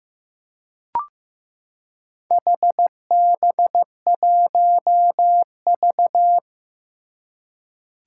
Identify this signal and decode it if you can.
Morse code, heard as a keyed tone, about 15 wpm, 700 Hz. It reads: HB1V
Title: Four Challenging Letter-Number Combinations 15wpm